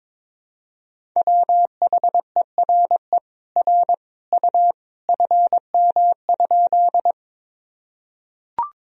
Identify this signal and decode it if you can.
Morse code, heard as a keyed tone, about 22 wpm, 700 Hz. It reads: WHERE R U FM?
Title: Sentences from Top 200 Words and Abbreviations 22wpm